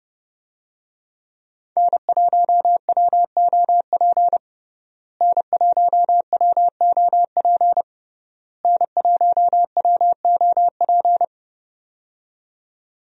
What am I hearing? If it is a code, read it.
Morse code, heard as a keyed tone, about 32 wpm, 700 Hz. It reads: N1WOP N1WOP N1WOP
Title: USA Call Signs - 5 Characters - Repeated 3x - 30wpm